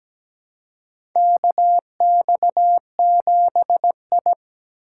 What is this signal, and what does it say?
Morse code, heard as a keyed tone, about 17 wpm, 700 Hz. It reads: KX7I
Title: USA Call Signs - 4 Characters - 3x Character Spacing - 17wpm